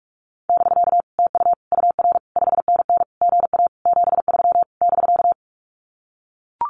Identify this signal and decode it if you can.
Morse code, heard as a keyed tone, about 44 wpm, 700 Hz. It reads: <BK> TU UR 5NN GA 73 <BK>
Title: Common QSO Phrases Encore 45wpm